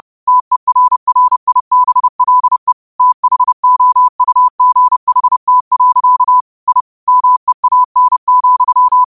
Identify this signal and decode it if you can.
Morse code, heard as a keyed tone, about 30 wpm, 1000 Hz. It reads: TERRIBLE THOUGHT. I MEAN,
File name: qotdfiles / 30wpm